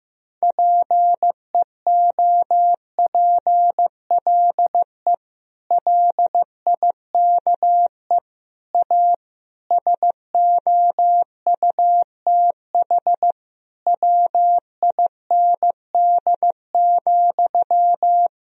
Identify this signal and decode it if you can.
Morse code, heard as a keyed tone, about 16 wpm, 700 Hz. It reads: PEOPLE LIKE A SOUTH WIND,
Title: Sentences from Top 1000 Words - Review Part 3 - 15wpm